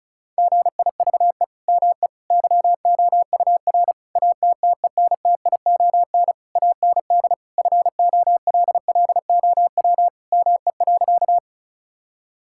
Morse code, heard as a keyed tone, about 36 wpm, 700 Hz. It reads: GIVE ME YOUR ATTENTION AND FOLLOW ME.
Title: Sentences from Top 1000 Words - Review Part 6 - 35wpm